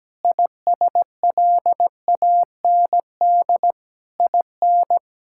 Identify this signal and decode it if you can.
Morse code, heard as a keyed tone, about 17 wpm, 700 Hz. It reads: ISLAND IN
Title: Sentences from Top 1000 Words - Review Part 6 - 17wpm